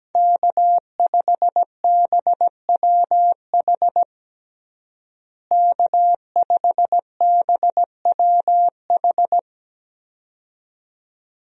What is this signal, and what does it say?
Morse code, heard as a keyed tone, about 17 wpm, 700 Hz. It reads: K5BWH K5BWH
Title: USA Call Signs - 5 Characters - Repeated 2x - 17wpm